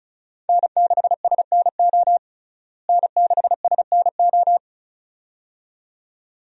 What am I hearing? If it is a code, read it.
Morse code, heard as a keyed tone, about 36 wpm, 700 Hz. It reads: N6SNO N6SNO
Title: USA Call Signs - 5 Characters - Repeated 2x - 35wpm